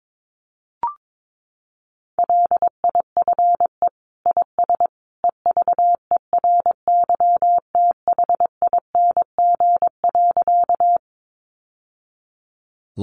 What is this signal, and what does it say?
Morse code, heard as a keyed tone, about 22 wpm, 700 Hz. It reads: LIFE IS EVERYTHING.
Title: Sentences from Top 700 Words 22wpm